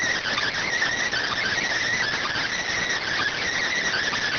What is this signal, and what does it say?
Emergency Alert System (SAME) tones, heard as harsh buzzing bursts. It is heard from 1.1 to 4.0 s.